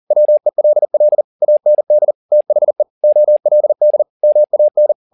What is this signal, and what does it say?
Morse code, heard as a keyed tone, about 40 wpm, 600 Hz. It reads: WELL AND THE OLD MAN
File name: the old man and the sea 2 40wpm